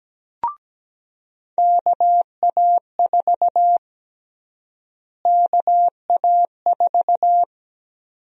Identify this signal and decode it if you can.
Morse code, heard as a keyed tone, about 17 wpm, 700 Hz. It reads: KA4 KA4
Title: USA Call Sign Prefixes - Repeated 2x - 17wpm